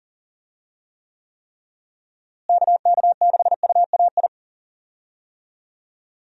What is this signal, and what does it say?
Morse code, heard as a keyed tone, about 40 wpm, 700 Hz. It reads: KK6UAI
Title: USA Call Signs - 6 Characters - Repeated 3x - 40wpm